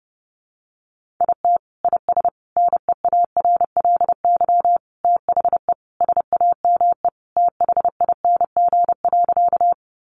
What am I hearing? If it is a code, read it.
Morse code, heard as a keyed tone, about 30 wpm, 700 Hz. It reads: IT IS NEARLY THE SAME THING.
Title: Sentences from Top 1300 Words 30wpm